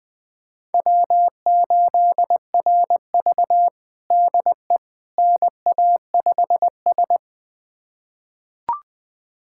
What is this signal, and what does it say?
Morse code, heard as a keyed tone, about 20 wpm, 700 Hz. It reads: W8RV DE NA5S
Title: USA Callsign DE Callsign - 4 Characters - 20wpm